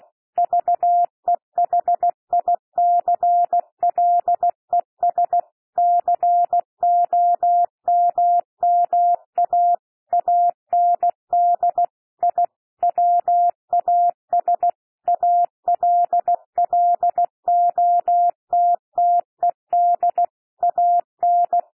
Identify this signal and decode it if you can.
Morse code, heard as a keyed tone, about 16 wpm, 700 Hz. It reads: VEHICLES COMMA AND I WAS ALLOTTED AN
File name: cwmp3 / P1 SOUND PASSAGE 16 wpm